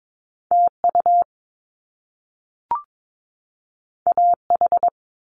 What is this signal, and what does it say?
Morse code, heard as a keyed tone, about 22 wpm, 700 Hz. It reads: TU AH
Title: Top Two Letter Combinations in Words 22wpm